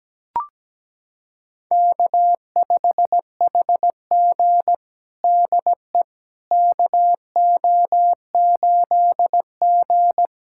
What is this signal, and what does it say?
Morse code, heard as a keyed tone, about 17 wpm, 700 Hz. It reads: K5HG DE KO8G
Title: USA Callsign DE Callsign - 4 Characters - 17wpm